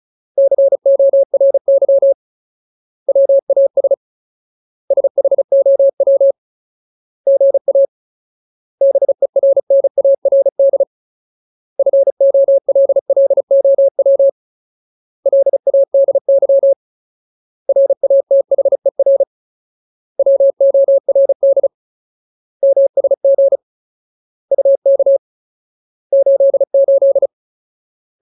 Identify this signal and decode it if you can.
Morse code, heard as a keyed tone, about 34 wpm, 550 Hz. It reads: CORY WAS SHOW GA BERNARD FOLLOW LADY RATHER WORD MSG UK 88
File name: Ultimate Word List / 35WPM Ultimate